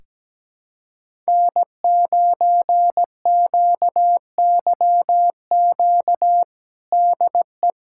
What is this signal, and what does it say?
Morse code, heard as a keyed tone, about 17 wpm, 700 Hz. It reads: N9QYQ DE
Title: USA Callsign DE Callsign - 5 Characters - 17wpm